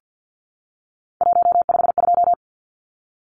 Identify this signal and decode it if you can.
Morse code, heard as a keyed tone, about 50 wpm, 700 Hz. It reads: 152
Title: Numbers - 3-Digit 50wpm